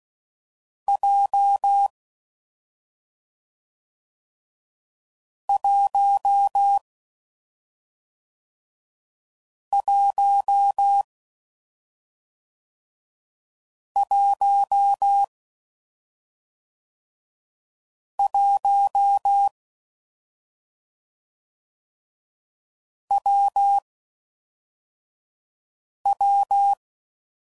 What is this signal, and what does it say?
Morse code, heard as a keyed tone, about 16 wpm, 800 Hz. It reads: J 1 1 1 1 W W
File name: Combinations / Difficult Combos / WPJ WPJ1